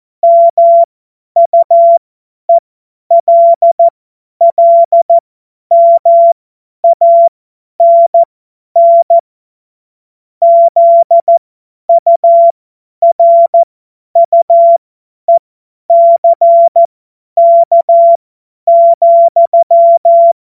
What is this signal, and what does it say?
Morse code, heard as a keyed tone, about 14 wpm, spaced as for 10 wpm, 680 Hz. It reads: MUELLMANN ZURUECK,